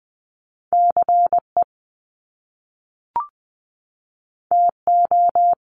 Morse code, heard as a keyed tone, about 20 wpm, 700 Hz. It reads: CE TO
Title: Top Two Letter Combinations in Words 20wpm